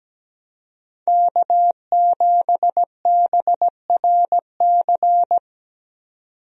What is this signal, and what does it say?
Morse code, heard as a keyed tone, about 17 wpm, 700 Hz. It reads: K7BRC
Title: USA Call Signs - 5 Characters - 17wpm